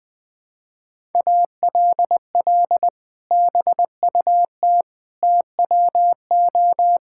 Morse code, heard as a keyed tone, about 20 wpm, 700 Hz. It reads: ALL BUT TWO
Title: Sentences from Top 1000 Words - Review Part 5 - 20wpm